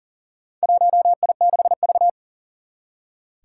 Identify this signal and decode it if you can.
Morse code, heard as a keyed tone, about 40 wpm, 700 Hz. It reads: 1I6V
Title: Four Challenging Letter-Number Combinations 40wpm